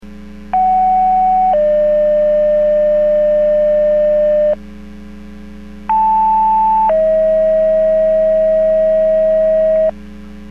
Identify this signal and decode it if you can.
Two-tone page, heard as a one-second tone, then a three-second tone, about 750 then 600 Hz, about 880 then 640 Hz. It is heard from 0.5 to 9.9 s.